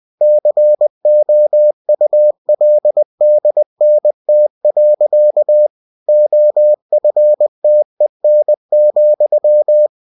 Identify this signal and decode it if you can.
Morse code, heard as a keyed tone, about 20 wpm, 600 Hz. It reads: COULDNT. OFTEN,
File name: cw-books / winnie-the-pooh / winnie-20wpm-page-0070